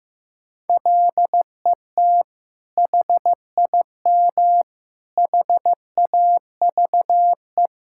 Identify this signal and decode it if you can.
Morse code, heard as a keyed tone, about 15 wpm, 700 Hz. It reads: LET HIM HAVE
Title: Sentences from Top 1600 Words 15wpm